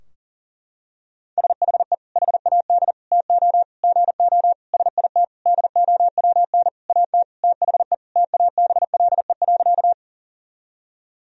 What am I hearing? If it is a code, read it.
Morse code, heard as a keyed tone, about 40 wpm, 700 Hz. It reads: SHE HAD TO GO SIT DOWN AT THE TABLE.
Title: Sentences from Top 1000 Words - Review Part 2 - 40wpm